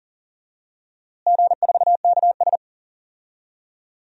Morse code, heard as a keyed tone, about 40 wpm, 700 Hz. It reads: G4KS
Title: DXCC-100 Call Signs - 4 Characters - Repeated 2x - 40wpm